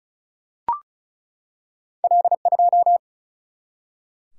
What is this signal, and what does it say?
Morse code, heard as a keyed tone, about 35 wpm, 700 Hz. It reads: L2
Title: Single-Letter Single-Number 35wpm